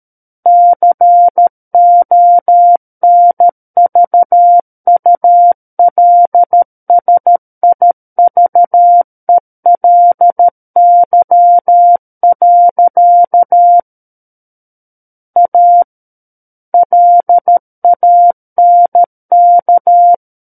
Morse code, heard as a keyed tone, about 13 wpm, 700 Hz. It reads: CONVULSIVELY. A LANK
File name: Book One / 04-Chapter 4 - 13 WPM